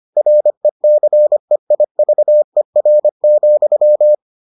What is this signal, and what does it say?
Morse code, heard as a keyed tone, about 25 wpm, 600 Hz. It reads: RECEIVER,